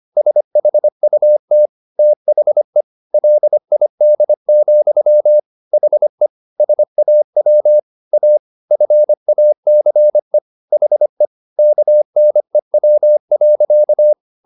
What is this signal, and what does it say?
Morse code, heard as a keyed tone, about 25 wpm, 600 Hz. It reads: SHUT THE LID, HE SAW A FACE HE KNEW.